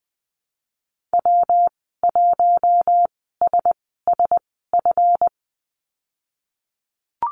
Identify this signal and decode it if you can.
Morse code, heard as a keyed tone, about 20 wpm, spaced as for 14 wpm, 700 Hz. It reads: W1SSF
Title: 147 Lesson 24g - Letter y - Call Signs 20wpm